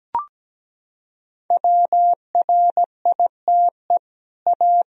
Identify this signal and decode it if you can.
Morse code, heard as a keyed tone, about 17 wpm, 700 Hz. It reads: WRITE A
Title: Sentences from Top 1000 Words - Review Part 3 - 17wpm